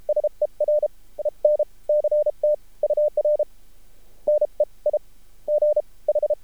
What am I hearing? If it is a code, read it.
Morse code, heard as a keyed tone, about 33 wpm, spaced as for 24 wpm, 600 Hz. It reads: SEREN CT UR NEIGH
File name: archived / CRSnet